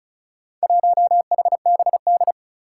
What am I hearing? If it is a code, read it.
Morse code, heard as a keyed tone, about 36 wpm, 700 Hz. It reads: 1HBD
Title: Four Challenging Letter-Number Combinations 35wpm